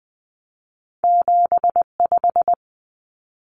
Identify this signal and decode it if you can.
Morse code, heard as a keyed tone, about 20 wpm, 700 Hz. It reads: 75
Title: Common Cut Numbers - 2-Digit 20wpm